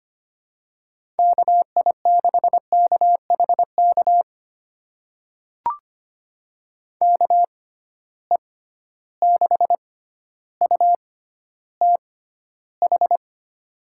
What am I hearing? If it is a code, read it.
Morse code, heard as a keyed tone, about 25 wpm, spaced as for 13 wpm, 700 Hz. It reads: KI6KHK KE6UTH
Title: USA Call Signs - 6 Characters - 6x Character Spacing - 25wpm